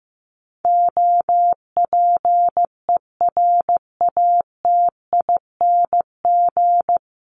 Morse code, heard as a keyed tone, about 15 wpm, 700 Hz. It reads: OPERATING
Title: ING Suffix - Top 100 Words with Base Words 15wpm